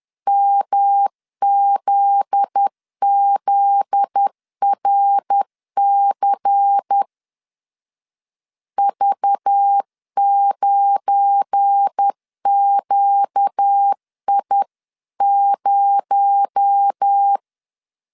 Morse code, heard as a keyed tone, about 10 wpm, 800 Hz. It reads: MZZRC V9QI0
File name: LEZIONI / 7 GF9Z / 7 GF9Z 10wpm 2611